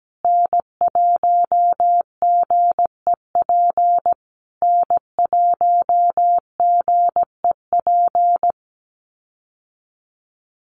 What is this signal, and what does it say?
Morse code, heard as a keyed tone, about 17 wpm, 700 Hz. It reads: N1GEP N1GEP
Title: USA Call Signs Repeated 3x 17wpm